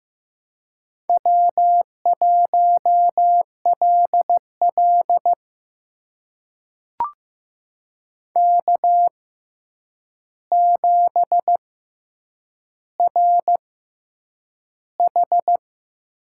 Morse code, heard as a keyed tone, about 15 wpm, spaced as for 8 wpm, 700 Hz. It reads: W1LL K7RH
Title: USA Call Signs - 4 Characters - 6x Character Spacing - 15wpm